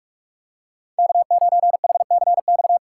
Morse code, heard as a keyed tone, about 46 wpm, 700 Hz. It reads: K9HCX
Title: USA Call Signs - 5 Characters - 4x Character Spacing - 45wpm